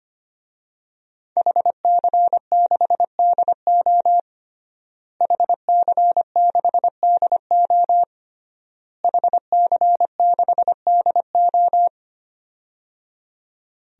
Morse code, heard as a keyed tone, about 25 wpm, 700 Hz. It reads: HC6DO HC6DO HC6DO
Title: DXCC-100 Call Signs - 5 Characters - Repeated 3x - 25wpm